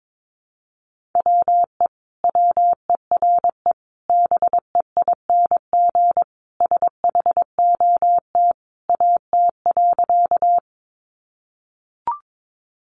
Morse code, heard as a keyed tone, about 22 wpm, 700 Hz. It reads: WE WERE BEING SHOT AT.
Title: Sentences from Top 1100 Words 22wpm